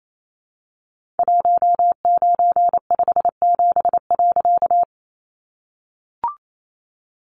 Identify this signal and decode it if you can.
Morse code, heard as a keyed tone, about 28 wpm, 700 Hz. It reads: 1957.
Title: A Sound Of Thunder 28wpm